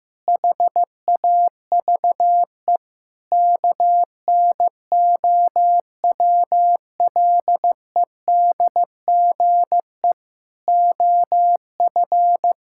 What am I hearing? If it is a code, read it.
Morse code, heard as a keyed tone, about 15 wpm, 700 Hz. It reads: HAVE KNOWLEDGE OF
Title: Sentences from Top 1000 Words - Review Part 6 - 15wpm